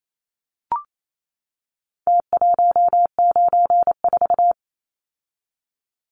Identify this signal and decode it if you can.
Morse code, heard as a keyed tone, about 28 wpm, 700 Hz. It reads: T194
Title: Numbers - Contest Style - 4-Digit 28wpm